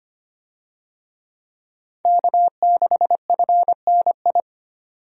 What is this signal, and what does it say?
Morse code, heard as a keyed tone, about 25 wpm, 700 Hz. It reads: K6FNI
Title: USA Call Signs - 5 Characters - 25wpm